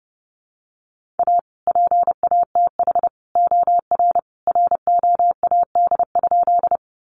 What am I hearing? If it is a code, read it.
Morse code, heard as a keyed tone, about 30 wpm, 700 Hz. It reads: A PATH OR ROAD?
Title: Sentences from Top 1400 Words 30wpm